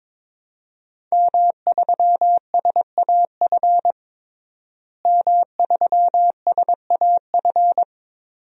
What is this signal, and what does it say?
Morse code, heard as a keyed tone, about 21 wpm, 700 Hz. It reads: M3SAF M3SAF
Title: DXCC-100 Call Signs - 5 Characters - Repeated 3x - 22wpm